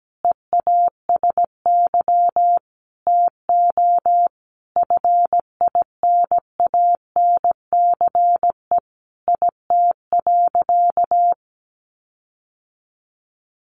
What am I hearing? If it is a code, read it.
Morse code, heard as a keyed tone, about 17 wpm, 700 Hz. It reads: EASY TO FINANCE IT.
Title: Sentences from Top 1500 Words 17wpm